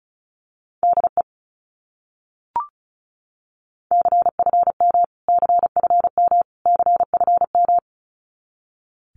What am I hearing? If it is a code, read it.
Morse code, heard as a keyed tone, about 34 wpm, 700 Hz. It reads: DE CFM CFM CFM
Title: Abbreviations Repeated 3x - 35wpm